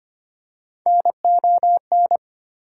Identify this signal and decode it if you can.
Morse code, heard as a keyed tone, about 25 wpm, 700 Hz. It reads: NON
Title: Prefixes and Suffixes 25wpm